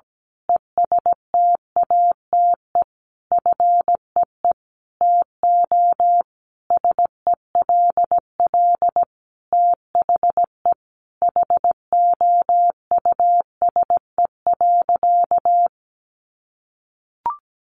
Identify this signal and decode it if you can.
Morse code, heard as a keyed tone, about 17 wpm, 700 Hz. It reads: ESTATE FEE TO SELL THE HOUSE.
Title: Sentences from Top 1500 Words 17wpm